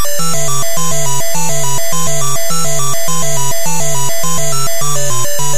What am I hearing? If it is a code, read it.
There is no code.